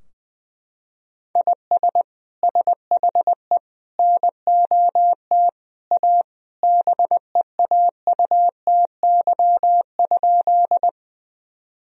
Morse code, heard as a keyed tone, about 21 wpm, 700 Hz. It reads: IS SHE NOT A BEAUTY?
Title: Sentences from Top 1000 Words - Review Part 2 - 20wpm